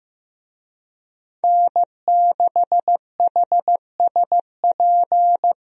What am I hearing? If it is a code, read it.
Morse code, heard as a keyed tone, about 15 wpm, 700 Hz. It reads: N6HSP
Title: USA Call Signs - 5 Characters - Repeated 2x - 15wpm